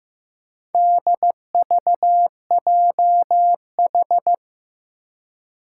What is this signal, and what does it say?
Morse code, heard as a keyed tone, about 15 wpm, 700 Hz. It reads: DVJH
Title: Four Challenging Letter-Number Combinations 15wpm